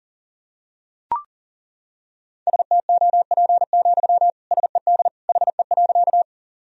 Morse code, heard as a keyed tone, about 37 wpm, 700 Hz. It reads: STOP, SED HE.
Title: Sentences from Top 300 Words and Abbreviations 40wpm